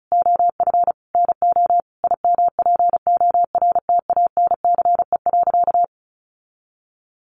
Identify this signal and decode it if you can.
Morse code, heard as a keyed tone, about 34 wpm, 700 Hz. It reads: OF NO IMPORTANCE.
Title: Sentences from Top 1200 Words 35wpm